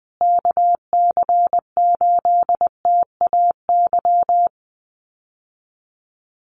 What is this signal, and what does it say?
Morse code, heard as a keyed tone, about 20 wpm, 700 Hz. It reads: KC8TAY
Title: USA Call Signs Repeated 2x 20wpm